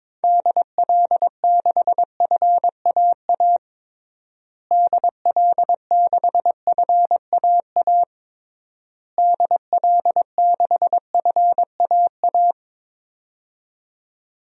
Morse code, heard as a keyed tone, about 22 wpm, 700 Hz. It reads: DL6FAA DL6FAA DL6FAA
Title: DXCC-100 Call Signs - 6 Characters - Repeated 3x - 22wpm